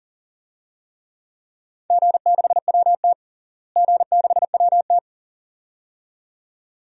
Morse code, heard as a keyed tone, about 40 wpm, 700 Hz. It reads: G6WT G6WT
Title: DXCC-100 Call Signs - 4 Characters - Repeated 2x - 40wpm